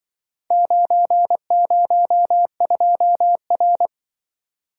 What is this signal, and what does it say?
Morse code, heard as a keyed tone, about 24 wpm, 700 Hz. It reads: 902R
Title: Speed-Racing - Four Letter-Number Combinations - 20wpm